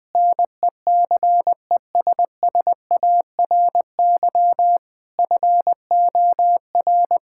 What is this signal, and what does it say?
Morse code, heard as a keyed tone, about 21 wpm, 700 Hz. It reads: NECESSARY FOR
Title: Sentences from Top 1000 Words - Review Part 5 - 20wpm